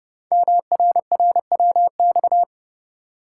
Morse code, heard as a keyed tone, about 30 wpm, 700 Hz. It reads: MRRWX
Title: Code Groups - 5 Letters - 30wpm